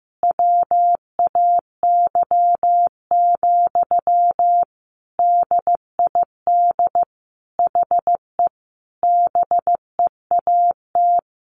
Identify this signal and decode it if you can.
Morse code, heard as a keyed tone, about 15 wpm, 700 Hz. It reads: WAY, DID HE BEAT
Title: Sentences from Top 1300 Words 15wpm